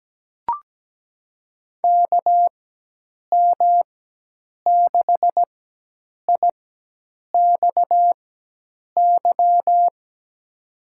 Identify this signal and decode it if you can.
Morse code, heard as a keyed tone, about 17 wpm, spaced as for 8 wpm, 700 Hz. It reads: KM6IXY
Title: USA Call Signs - 6 Characters - 4x Character Spacing - 17wpm